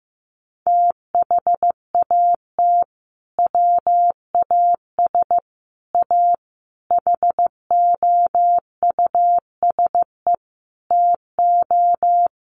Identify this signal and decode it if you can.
Morse code, heard as a keyed tone, about 15 wpm, 700 Hz. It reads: THAT WAS A HOUSE TO
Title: Sentences from Top 200 Words 15wpm